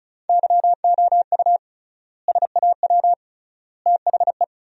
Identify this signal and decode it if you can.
Morse code, heard as a keyed tone, about 36 wpm, 700 Hz. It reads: YOU SAW THE
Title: 3x Word-Spacing - Sentences from Top 400 Words - 35wpm